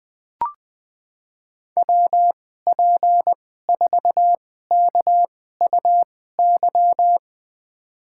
Morse code, heard as a keyed tone, about 21 wpm, spaced as for 15 wpm, 700 Hz. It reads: WP4KUY
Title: USA Call Signs - 2x Character Spacing - 20wpm